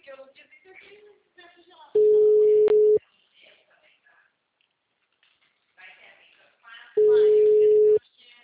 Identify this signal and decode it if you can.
Ringback tone (Europe, 425 Hz), with speech.